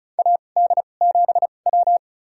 Morse code, heard as a keyed tone, about 34 wpm, spaced as for 25 wpm, 700 Hz. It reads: AD7W
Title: USA Call Signs - 2x Character Spacing - 35wpm